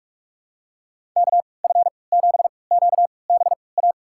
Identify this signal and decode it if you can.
Morse code, heard as a keyed tone, about 44 wpm, spaced as for 25 wpm, 700 Hz. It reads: KF7QBA
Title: USA Call Signs - 6 Characters - 3x Character Spacing - 45wpm